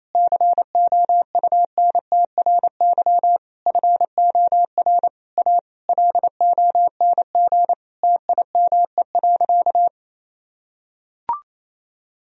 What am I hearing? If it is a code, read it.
Morse code, heard as a keyed tone, about 28 wpm, 700 Hz. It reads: COUNTRY FOR A LONG TIME.
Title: Sentences from Top 1000 Words - Review Part 1 - 28wpm